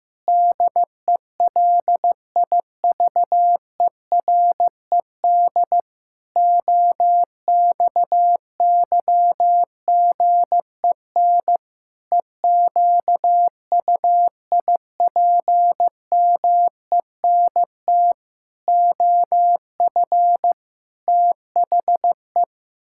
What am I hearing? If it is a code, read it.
Morse code, heard as a keyed tone, about 16 wpm, 700 Hz. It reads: DELIVERED OXYGEN EQUIPMENT OF THE
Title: Pangram Challenge v2 - 15wpm